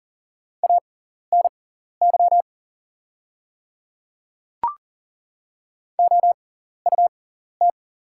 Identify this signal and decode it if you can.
Morse code, heard as a keyed tone, about 40 wpm, spaced as for 14 wpm, 700 Hz. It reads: ANY OUT
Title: 6x Character Spacing - Top 200 Words - 40wpm